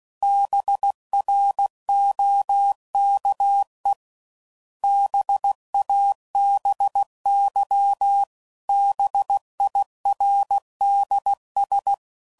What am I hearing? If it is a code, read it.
Morse code, heard as a keyed tone, about 16 wpm, 800 Hz. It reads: BROKE BABY BIRDS